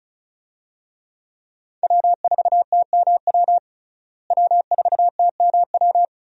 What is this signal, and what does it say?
Morse code, heard as a keyed tone, about 36 wpm, 700 Hz. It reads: W4TMW W4TMW
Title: USA Call Signs - 5 Characters - Repeated 2x - 35wpm